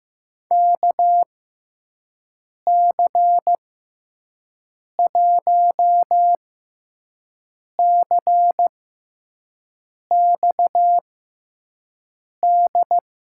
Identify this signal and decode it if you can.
Morse code, heard as a keyed tone, about 15 wpm, spaced as for 5 wpm, 700 Hz. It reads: KC1CXD
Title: USA Call Signs - 6 Characters - 6x Character Spacing - 15wpm